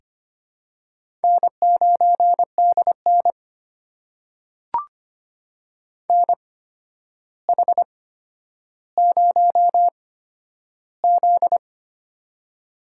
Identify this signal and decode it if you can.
Morse code, heard as a keyed tone, about 25 wpm, spaced as for 11 wpm, 700 Hz. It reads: N9DN NH0Z
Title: USA Call Signs - 4 Characters - 8x Character Spacing - 25wpm